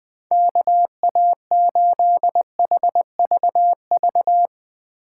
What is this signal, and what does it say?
Morse code, heard as a keyed tone, about 20 wpm, 700 Hz. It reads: KA8HVV
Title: USA Call Signs - 6 Characters - 3x Character Spacing - 20wpm